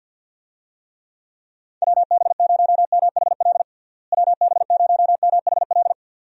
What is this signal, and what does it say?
Morse code, heard as a keyed tone, about 46 wpm, 700 Hz. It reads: WB0MHL WB0MHL
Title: USA Call Signs - 6 Characters - Repeated 2x - 50wpm